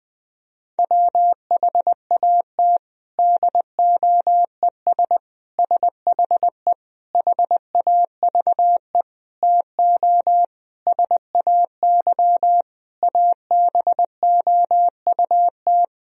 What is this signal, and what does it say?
Morse code, heard as a keyed tone, about 20 wpm, 700 Hz. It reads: WHAT DOES SHE HAVE TO SAY ABOUT
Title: Sentences from Top 1000 Words - Review Part 3 - 20wpm